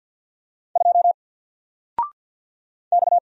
Morse code, heard as a keyed tone, about 48 wpm, 700 Hz. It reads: 2 X
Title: Single Character (Letter:Number:Punctuation:Prosign) Alt 50wpm